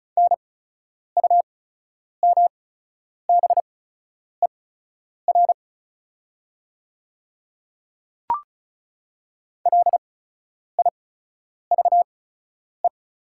Morse code, heard as a keyed tone, about 36 wpm, spaced as for 10 wpm, 690 Hz. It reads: NUMBER LIVE
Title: 8x Character Spacing - Top 500 Words - 35wpm